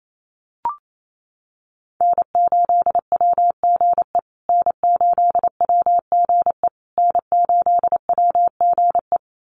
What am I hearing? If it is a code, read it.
Morse code, heard as a keyed tone, about 28 wpm, 700 Hz. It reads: N8WGE N8WGE N8WGE
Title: USA Call Signs Repeated 3x 28wpm